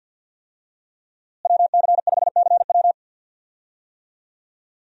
Morse code, heard as a keyed tone, about 49 wpm, 700 Hz. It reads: WC5CW
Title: USA Call Signs - 5 Characters - Repeated 3x - 50wpm